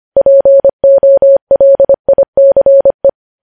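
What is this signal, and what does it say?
Morse code, heard as a keyed tone, about 25 wpm, 550 Hz. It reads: POLICE
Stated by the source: Morse code news headlines sent at 25 wpm.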